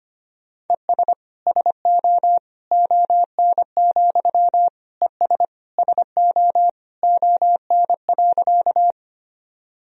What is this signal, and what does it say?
Morse code, heard as a keyed tone, about 25 wpm, 700 Hz. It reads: ES SO ON, ES SO ON.
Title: Sentences from Top 100 Words and Abbreviations 25wpm